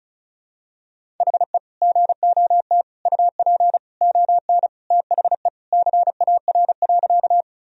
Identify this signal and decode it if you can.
Morse code, heard as a keyed tone, about 36 wpm, 700 Hz. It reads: HE GOT UP ON THE CAR.
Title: Sentences from Top 1000 Words - Review Part 1 - 35wpm